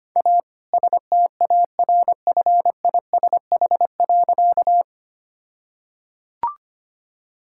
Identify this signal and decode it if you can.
Morse code, heard as a keyed tone, about 25 wpm, 700 Hz. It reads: A STARFISH.
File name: practice / 25wpm / All-Age Appropriate Jokes - 25wpm